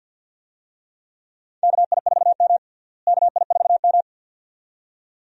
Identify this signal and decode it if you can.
Morse code, heard as a keyed tone, about 51 wpm, 700 Hz. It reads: KI4M KI4M
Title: USA Call Signs - 4 Characters - Repeated 2x - 50wpm